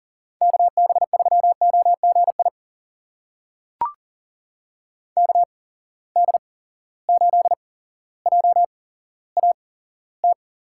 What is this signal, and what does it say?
Morse code, heard as a keyed tone, about 38 wpm, spaced as for 17 wpm, 700 Hz. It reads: KB3OGI KD8JAT
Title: USA Call Signs - 6 Characters - 8x Character Spacing - 40wpm